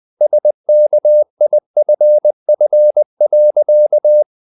Morse code, heard as a keyed tone, about 20 wpm, 600 Hz. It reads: SKIFF.
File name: the old man and the sea part 2 20wpm